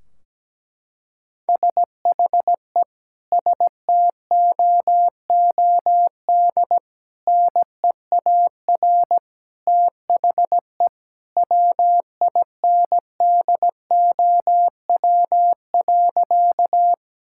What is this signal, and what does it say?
Morse code, heard as a keyed tone, about 17 wpm, 700 Hz. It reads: SHE STOOD NEAR THE WINDOW.
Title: Sentences from Top 1600 Words 17wpm